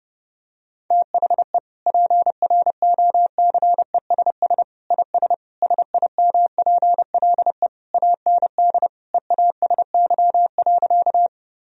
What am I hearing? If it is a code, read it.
Morse code, heard as a keyed tone, about 30 wpm, 700 Hz. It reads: THE PROCESS IS SIMPLE AND EASY.
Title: Sentences from Top 1000 Words - Review Part 3 - 30wpm